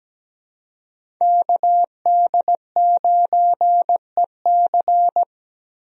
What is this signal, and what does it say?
Morse code, heard as a keyed tone, about 17 wpm, 700 Hz. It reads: KD9EC
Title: USA Call Signs - 5 Characters - Repeated 2x - 17wpm